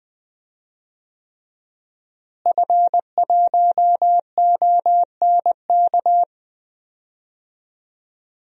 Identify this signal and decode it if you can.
Morse code, heard as a keyed tone, about 20 wpm, 700 Hz. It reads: F1ONK
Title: DXCC-100 Call Signs - 5 Characters - Repeated 3x - 20wpm